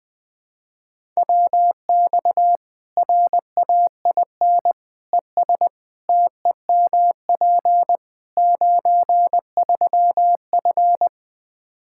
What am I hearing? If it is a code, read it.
Morse code, heard as a keyed tone, about 20 wpm, 700 Hz. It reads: WX RAIN ES TEMP 93F
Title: Practice QSO - Exchange 2 - Info - 20wpm